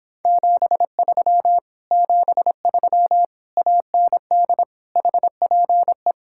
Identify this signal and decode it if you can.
Morse code, heard as a keyed tone, about 26 wpm, 700 Hz. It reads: 73 73 AND HPE